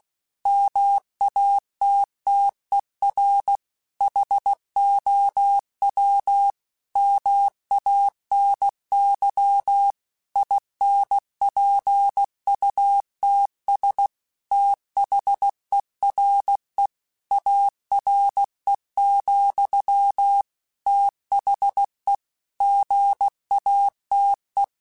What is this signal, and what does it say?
Morse code, heard as a keyed tone, about 16 wpm, 800 Hz. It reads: MATTER HOW MANY INPUTS THERE ARE, THE GATE
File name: Technical Subjects / Technical Subjects-mp3s / OR Gate @15